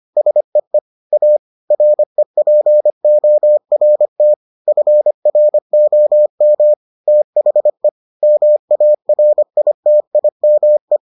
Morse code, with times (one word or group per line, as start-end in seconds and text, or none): 0.2-0.8 SEE
1.1-1.4 A
1.7-4.3 REPORT
4.7-6.7 FROM
7.1-7.9 THE
8.2-11.0 MARITIME